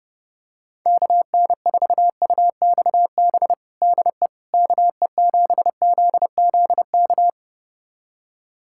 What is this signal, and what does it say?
Morse code, heard as a keyed tone, about 32 wpm, 700 Hz. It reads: KN4UXB DE KE7ZZK
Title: USA Callsign DE Callsign - 6 Characters - 30wpm